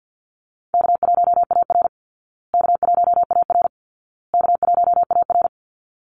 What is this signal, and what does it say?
Morse code, heard as a keyed tone, about 50 wpm, 700 Hz. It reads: K1AR K1AR K1AR
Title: Easy Call Signs Repeated 3x - 50wpm